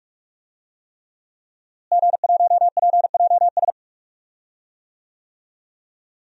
Morse code, heard as a keyed tone, about 46 wpm, 700 Hz. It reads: G1PJS
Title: DXCC-100 Call Signs - 5 Characters - Repeated 3x - 45wpm